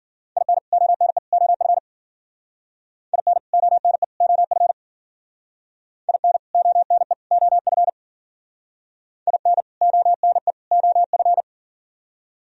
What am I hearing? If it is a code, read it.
Morse code, heard as a keyed tone, about 48 wpm, 700 Hz. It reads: IN ONE OF IN ONE OF IN ONE OF IN ONE OF
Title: Speed-Racing - Sets of 3 Words 40wpm